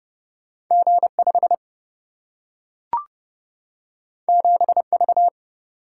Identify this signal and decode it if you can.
Morse code, heard as a keyed tone, about 29 wpm, 700 Hz. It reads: G 5 7 V
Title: Single-Letter Single-Number 30wpm